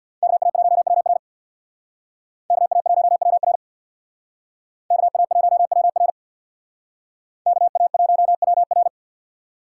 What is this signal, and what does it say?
Morse code, heard as a keyed tone, about 64 wpm, 700 Hz. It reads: KA1WR KA1WR KA1WR KA1WR